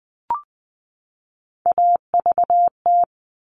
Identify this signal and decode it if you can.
Morse code, heard as a keyed tone, about 20 wpm, 700 Hz. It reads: AVT
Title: All Cut Numbers - 3-Digit 20wpm